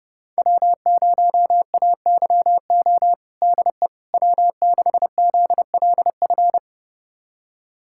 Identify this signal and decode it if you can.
Morse code, heard as a keyed tone, about 30 wpm, 700 Hz. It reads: W0AYO DE W6ZLF
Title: USA Callsign DE Callsign - 5 Characters - 30wpm